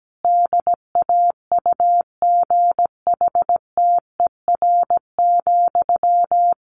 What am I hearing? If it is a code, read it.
Morse code, heard as a keyed tone, about 17 wpm, 700 Hz. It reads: DAUGHTER,
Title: Sentences from Top 1500 Words 17wpm